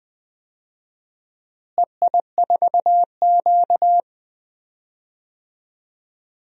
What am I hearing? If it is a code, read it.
Morse code, heard as a keyed tone, about 20 wpm, 700 Hz. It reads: EI4Q
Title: DXCC-100 Call Signs - 4 Characters - Repeated 3x - 20wpm